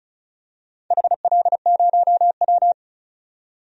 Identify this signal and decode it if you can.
Morse code, heard as a keyed tone, about 34 wpm, 700 Hz. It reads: HL0W
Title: DXCC-100 Call Signs - 4 Characters - 35wpm